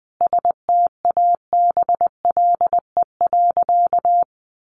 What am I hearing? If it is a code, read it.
Morse code, heard as a keyed tone, about 20 wpm, 700 Hz. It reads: STABLE.
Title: Sentences from Top 1500 Words 20wpm